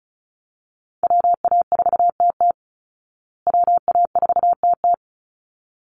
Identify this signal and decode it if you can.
Morse code, heard as a keyed tone, about 34 wpm, 700 Hz. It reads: WA4TT WA4TT
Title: Call Signs Repeated 3x - 35wpm